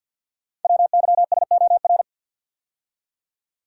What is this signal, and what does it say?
Morse code, heard as a keyed tone, about 48 wpm, 700 Hz. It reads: WYSOR
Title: Code Groups - 5 Letters - 50wpm